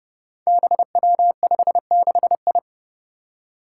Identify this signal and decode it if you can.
Morse code, heard as a keyed tone, about 30 wpm, 700 Hz. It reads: BW56I
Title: Five Challenging Letter-Number Combinations 30wpm